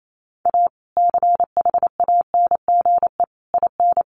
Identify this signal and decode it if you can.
Morse code, heard as a keyed tone, about 28 wpm, 700 Hz. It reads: A CHANGE IN
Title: Sentences from Top 1500 Words 28wpm